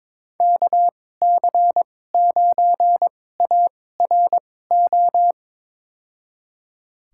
Morse code, heard as a keyed tone, about 22 wpm, spaced as for 16 wpm, 700 Hz. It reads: KC9ARO
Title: USA Call Signs - 2x Character Spacing - 22wpm